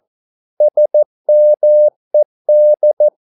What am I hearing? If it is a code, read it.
Morse code, heard as a keyed tone, about 14 wpm, 600 Hz. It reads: SMED
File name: Korte ord DK 14wpm